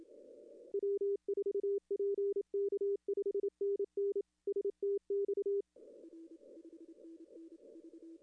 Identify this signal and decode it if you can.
Morse code, heard as a keyed tone, about 27 wpm, 410 Hz. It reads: W4PK 5NN STX
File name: contests / arrl-160-k5tr-op-w5tn -2300